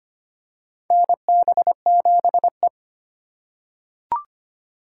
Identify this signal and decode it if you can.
Morse code, heard as a keyed tone, about 25 wpm, 700 Hz. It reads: NB7E
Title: USA Call Signs - 4 Characters - 3x Character Spacing - 25wpm